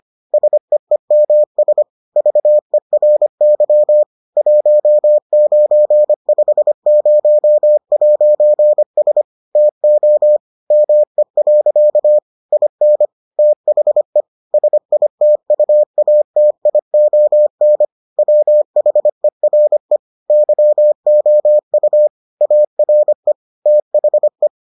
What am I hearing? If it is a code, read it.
Morse code, heard as a keyed tone, about 25 wpm, 600 Hz. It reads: SEEMS VERY 1950'S TO ME. IN THE SITUATION WHERE YOU ARE THE